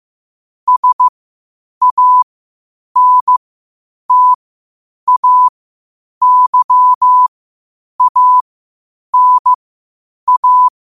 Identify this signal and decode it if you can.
Morse code, heard as a keyed tone, about 15 wpm, spaced as for 8 wpm, 1000 Hz. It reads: SANTAYANA
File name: qotdfiles / 8wpm